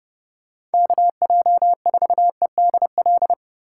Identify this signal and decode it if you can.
Morse code, heard as a keyed tone, about 31 wpm, 700 Hz. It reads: KJ4EDL